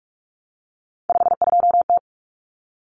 Morse code, heard as a keyed tone, about 44 wpm, 700 Hz. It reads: 52T
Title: Common Cut Numbers - 3-Digit 45wpm